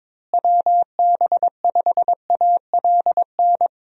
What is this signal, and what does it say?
Morse code, heard as a keyed tone, about 22 wpm, 700 Hz. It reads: WB5ALN
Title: USA Call Signs - 6 Characters - Repeated 2x - 22wpm